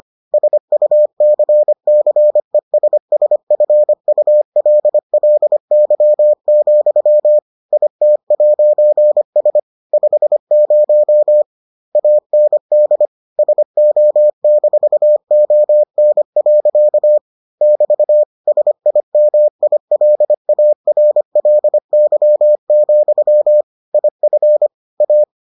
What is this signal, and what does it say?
Morse code, heard as a keyed tone, about 25 wpm, 600 Hz. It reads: SUCCESSFULLY, IT'S 50 AND SO-ON. <BT> SIMILARLY, IF A